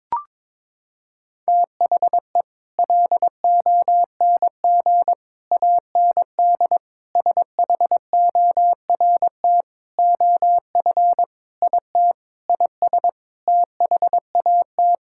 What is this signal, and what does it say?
Morse code, heard as a keyed tone, about 22 wpm, 700 Hz. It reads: THE LONG AND SHORT OF IT IS THAT
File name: practice / 22wpm / Sentences with 8-Word Phrases - 22wpm